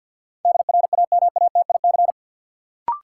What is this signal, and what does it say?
Morse code, heard as a keyed tone, about 53 wpm, 700 Hz. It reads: DRAMATIC
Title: Sentences from Top 1300 Words - New Word Drill Alt - 50wpm